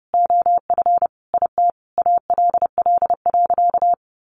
Morse code, heard as a keyed tone, about 28 wpm, 700 Hz. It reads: OF IT ALL.
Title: Sentences from Top 1500 Words 30wpm